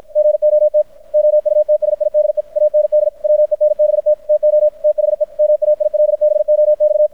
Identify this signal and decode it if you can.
Morse code, heard as a keyed tone, about 53 wpm, 600 Hz. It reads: GOT OUTSIDE AND WENBT TO THE MAILBOX
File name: archived / CRSnet